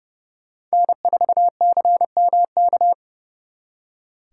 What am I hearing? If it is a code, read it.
Morse code, heard as a keyed tone, about 36 wpm, 700 Hz. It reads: N4CMK
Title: Speed Racing - USA Call Signs 30wpm